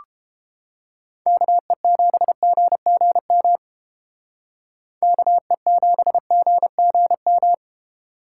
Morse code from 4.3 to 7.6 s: KE7GGM

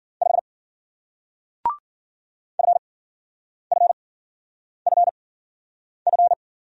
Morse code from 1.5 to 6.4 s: FFFF

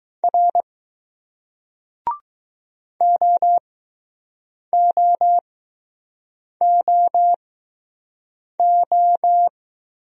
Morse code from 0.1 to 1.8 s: R

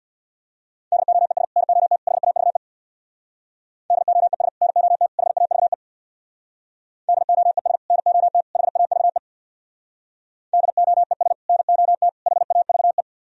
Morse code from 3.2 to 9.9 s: DOES NOT HAVE DOES NOT HAVE

